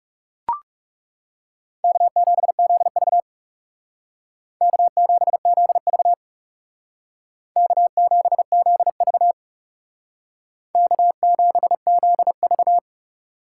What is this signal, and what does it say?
Morse code, heard as a keyed tone, about 36 wpm, 700 Hz. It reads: K7ZV K7ZV K7ZV K7ZV
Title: Speed Racing - USA 1x2 Call Signs 30wpm